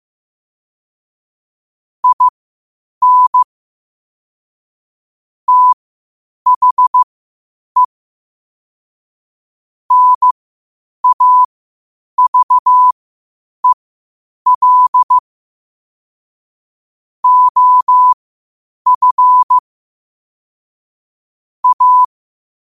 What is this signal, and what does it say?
Morse code, heard as a keyed tone, about 15 wpm, spaced as for 8 wpm, 1000 Hz. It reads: IN THE NAVEL OF A
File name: qotdfiles / 8wpm